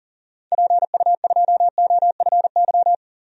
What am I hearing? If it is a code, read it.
Morse code, heard as a keyed tone, about 40 wpm, 700 Hz. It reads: PU2OFY
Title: DXCC-100 Call Signs - 6 Characters - 40wpm